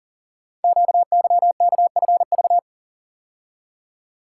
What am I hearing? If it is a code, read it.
Morse code, heard as a keyed tone, about 38 wpm, 700 Hz. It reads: QYKFV
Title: Code Groups - 5 Letters - 40wpm